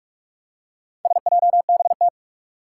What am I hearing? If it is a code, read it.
Morse code, heard as a keyed tone, about 41 wpm, 700 Hz. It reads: SJBT
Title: Four Challenging Letter-Number Combinations 45wpm